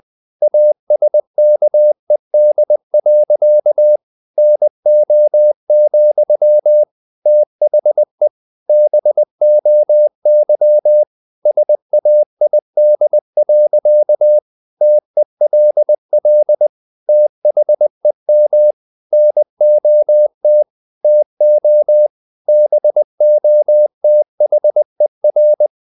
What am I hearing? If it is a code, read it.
Morse code, heard as a keyed tone, about 20 wpm, 600 Hz. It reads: ASKED. NO, THE BOY SAID. TELL THEM NOT TO BOTHER
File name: the old man and the sea part 2 20wpm